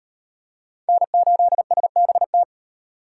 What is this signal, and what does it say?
Morse code, heard as a keyed tone, about 36 wpm, 700 Hz. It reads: N8SBT